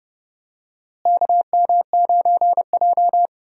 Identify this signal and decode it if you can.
Morse code, heard as a keyed tone, about 31 wpm, 700 Hz. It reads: KM9J
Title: USA Call Signs - 4 Characters - Repeated 3x - 30wpm